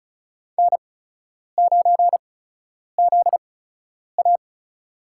Morse code, heard as a keyed tone, about 36 wpm, spaced as for 10 wpm, 700 Hz. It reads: N9ZA
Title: USA Call Signs - 4 Characters - 8x Character Spacing - 35wpm